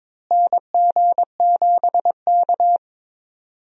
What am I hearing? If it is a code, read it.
Morse code, heard as a keyed tone, about 22 wpm, 700 Hz. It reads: NG7K
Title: USA Call Signs - 4 Characters - Repeated 2x - 22wpm